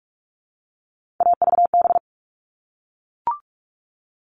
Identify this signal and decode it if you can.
Morse code, heard as a keyed tone, about 44 wpm, 700 Hz. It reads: AVB
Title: All Cut Numbers - 3-Digit 45wpm